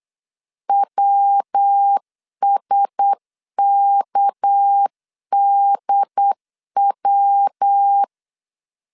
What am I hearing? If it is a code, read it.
Morse code, heard as a keyed tone, about 8 wpm, 800 Hz. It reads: WSKDW
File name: CW / DET / 1 6 1500C S C 8wpm 5219